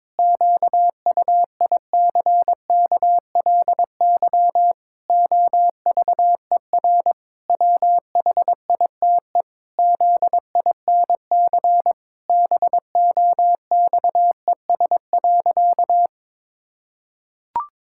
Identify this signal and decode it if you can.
Morse code, heard as a keyed tone, about 22 wpm, 700 Hz. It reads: QUICKLY OVER WHITE ZINC BOXES.
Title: Pangram Challenge v2 - 22wpm